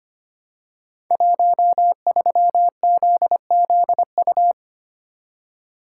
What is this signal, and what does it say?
Morse code, heard as a keyed tone, about 25 wpm, 700 Hz. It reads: 13ZZU
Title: Code Groups - 5 Letter:Numbers - 25wpm